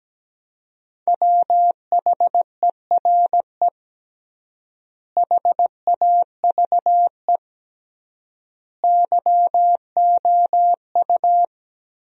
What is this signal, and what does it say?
Morse code, heard as a keyed tone, about 17 wpm, 700 Hz. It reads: WHERE HAVE YOU
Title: 3x Word-Spacing - Sentences from Top 300 Words - 17wpm